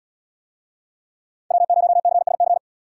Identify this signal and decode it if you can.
Morse code, heard as a keyed tone, about 73 wpm, 700 Hz. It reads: W0CSC